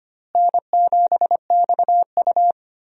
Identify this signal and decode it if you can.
Morse code, heard as a keyed tone, about 25 wpm, 700 Hz. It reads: N7XU
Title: USA Call Signs - 4 Characters - Repeated 3x - 25wpm